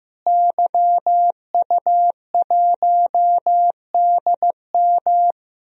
Morse code, heard as a keyed tone, about 15 wpm, 700 Hz. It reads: YU1DM
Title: DXCC-100 Call Signs - 5 Characters - Repeated 3x - 15wpm